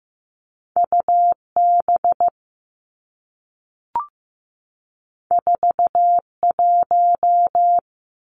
Morse code, heard as a keyed tone, about 15 wpm, 700 Hz. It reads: U B 4 1
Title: Two Challenging Letter-Number Combinations 15wpm